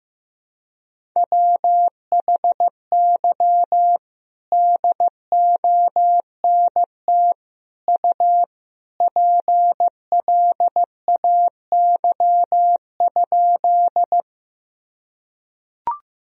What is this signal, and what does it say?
Morse code, heard as a keyed tone, about 15 wpm, 700 Hz. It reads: WHY DONT U PLAY?
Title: Sentences from Top 300 Words and Abbreviations 15wpm